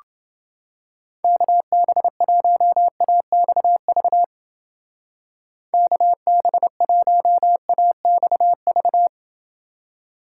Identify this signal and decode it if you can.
Morse code, heard as a keyed tone, about 29 wpm, 700 Hz. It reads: KB1AXV KB1AXV